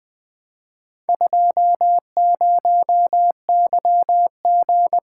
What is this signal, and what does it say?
Morse code, heard as a keyed tone, about 20 wpm, 700 Hz. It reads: 20YG
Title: Four Letter-Number Combinations - 20wpm